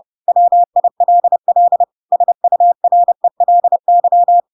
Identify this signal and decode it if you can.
Morse code, heard as a keyed tone, about 30 wpm, 700 Hz. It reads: WILL SURELY
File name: Pinocchio / Pinocchio ch6 30 30 ik1hge